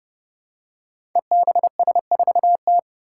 Morse code, heard as a keyed tone, about 30 wpm, 700 Hz. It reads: EBS4T
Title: Five Challenging Letter-Number Combinations 30wpm